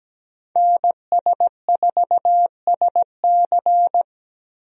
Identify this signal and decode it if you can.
Morse code, heard as a keyed tone, about 17 wpm, 700 Hz. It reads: NS4SC
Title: USA Call Signs - 5 Characters - Repeated 2x - 17wpm